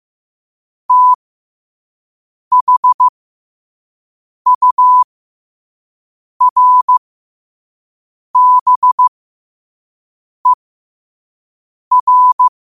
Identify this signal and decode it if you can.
Morse code, heard as a keyed tone, about 15 wpm, spaced as for 5 wpm, 1000 Hz. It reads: THURBER
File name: qotdfiles / 5wpm